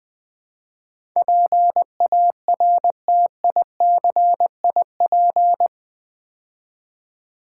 Morse code, heard as a keyed tone, about 20 wpm, 700 Hz. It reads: PARTICIP
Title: Build A Word - 4 Letters At A Time - 11-Letter Words - 20wpm